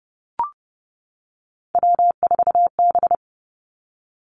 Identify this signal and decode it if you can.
Morse code, heard as a keyed tone, about 30 wpm, 700 Hz. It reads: W4B
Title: Three Challenging Letter-Number Combinations 30wpm